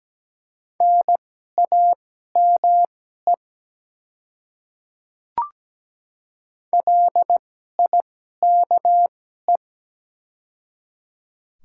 Morse code, heard as a keyed tone, about 17 wpm, spaced as for 12 wpm, 700 Hz. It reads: NAME LIKE
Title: 2x Character Spacing - Top 300 Words - 17wpm